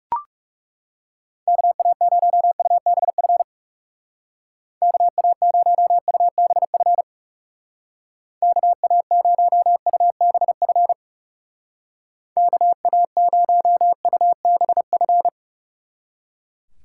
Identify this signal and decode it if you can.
Morse code, heard as a keyed tone, about 38 wpm, 700 Hz. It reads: KA0UBF KA0UBF KA0UBF KA0UBF